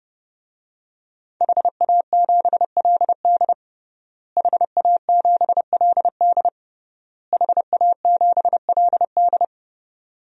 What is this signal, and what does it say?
Morse code, heard as a keyed tone, about 30 wpm, 700 Hz. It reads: HA7LD HA7LD HA7LD
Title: DXCC-100 Call Signs - 5 Characters - Repeated 3x - 30wpm